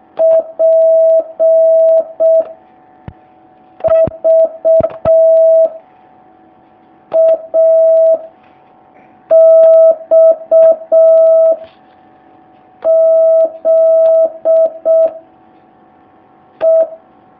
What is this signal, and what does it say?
Morse code, heard as a keyed tone, about 6 wpm, 650 Hz. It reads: P V A X Z E
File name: jh8cbh / 2019cw